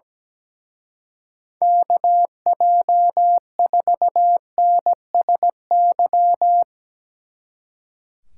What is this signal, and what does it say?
Morse code, heard as a keyed tone, about 18 wpm, 700 Hz. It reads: KJ4NSY
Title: Speed Racing - USA Call Signs 17wpm